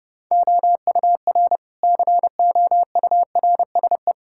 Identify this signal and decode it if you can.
Morse code, heard as a keyed tone, about 30 wpm, 700 Hz. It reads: OUR COURSE
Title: Sentences from Top 1000 Words - Review Part 6 - 30wpm